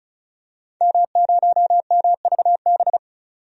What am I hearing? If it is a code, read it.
Morse code, heard as a keyed tone, about 34 wpm, 700 Hz. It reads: M0MVB
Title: DXCC-100 Call Signs - 5 Characters - 35wpm